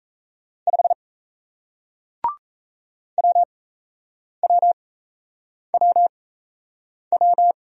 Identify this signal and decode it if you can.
Morse code, heard as a keyed tone, about 39 wpm, spaced as for 8 wpm, 700 Hz. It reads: 5 WWWW